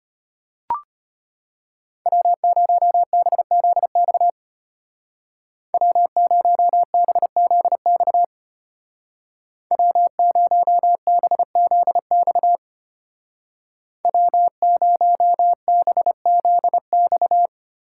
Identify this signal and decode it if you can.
Morse code, heard as a keyed tone, about 31 wpm, 700 Hz. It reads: W0BZX W0BZX W0BZX W0BZX
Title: Speed Racing - USA 1x3 Call Signs 25wpm